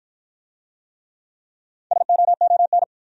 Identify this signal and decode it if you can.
Morse code, heard as a keyed tone, about 53 wpm, 700 Hz. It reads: SOON